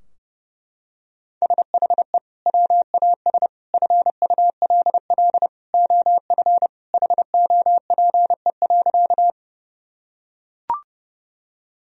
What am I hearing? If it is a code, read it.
Morse code, heard as a keyed tone, about 32 wpm, 700 Hz. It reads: SHE WAS FULL OF HOPE.
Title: Sentences from Top 1000 Words - Review Part 3 - 30wpm